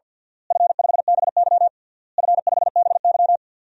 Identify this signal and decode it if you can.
Morse code, heard as a keyed tone, about 47 wpm, 700 Hz. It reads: F5BY F5BY
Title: DXCC-100 Call Signs - 4 Characters - Repeated 3x - 50wpm